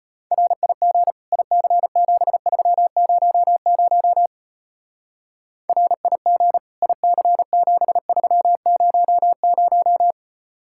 Morse code, heard as a keyed tone, about 29 wpm, 700 Hz. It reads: RIG IC7300 RIG IC7300
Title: Speed-Racing Common QSO Phrases Encore - 25wpm